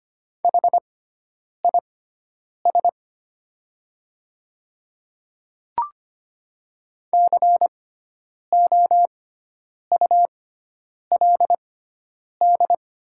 Morse code, heard as a keyed tone, about 25 wpm, spaced as for 9 wpm, 700 Hz. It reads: HIS COULD
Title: 6x Character Spacing - Top 100 Words - 25wpm